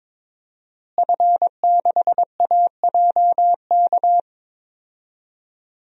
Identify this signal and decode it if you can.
Morse code, heard as a keyed tone, about 22 wpm, 700 Hz. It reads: F6AJK
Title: DXCC-100 Call Signs - 5 Characters - Repeated 3x - 22wpm